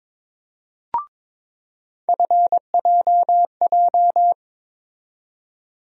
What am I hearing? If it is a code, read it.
Morse code, heard as a keyed tone, about 22 wpm, 700 Hz. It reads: FJJ
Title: USA Call Sign Postfixes - 3 Letters 22wpm